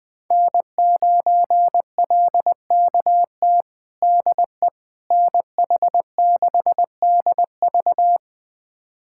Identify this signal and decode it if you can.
Morse code, heard as a keyed tone, about 20 wpm, 700 Hz. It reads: N9LKT DE NH6DV
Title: USA Callsign DE Callsign - 5 Characters - 20wpm